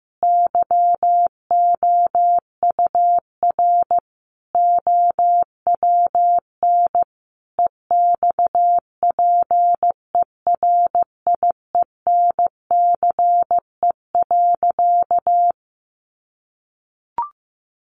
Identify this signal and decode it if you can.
Morse code, heard as a keyed tone, about 14 wpm, 700 Hz. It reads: YOUR OWN EXPERIENCE.
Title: Sentences from Top 600 Words 15wpm